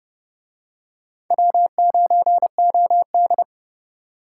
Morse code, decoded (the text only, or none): W9OD